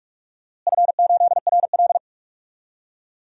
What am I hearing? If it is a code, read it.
Morse code, heard as a keyed tone, about 46 wpm, 700 Hz. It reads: F8RL
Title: DXCC-100 Call Signs - 4 Characters - Repeated 2x - 45wpm